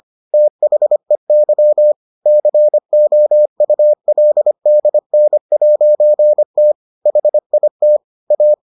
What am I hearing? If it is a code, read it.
Morse code, heard as a keyed tone, about 25 wpm, 600 Hz. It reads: THEY COULDN'T HIT A